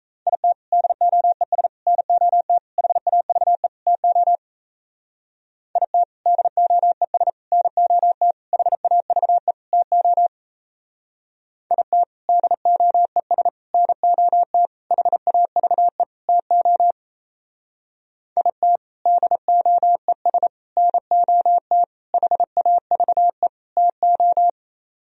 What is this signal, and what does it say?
Morse code, heard as a keyed tone, about 35 wpm, 700 Hz. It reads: IT DOES NOT HAVE TO IT DOES NOT HAVE TO IT DOES NOT HAVE TO IT DOES NOT HAVE TO
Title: Speed-Racing - Sets of 5 Words 28wpm